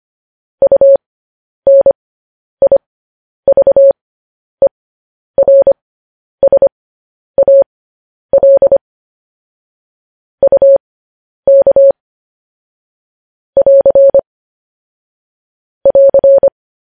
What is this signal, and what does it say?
Morse code, heard as a keyed tone, about 25 wpm, spaced as for 10 wpm, 550 Hz. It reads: UNIVERSAL UK <AR> <AR>
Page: news headlines in morse code at 10 wpm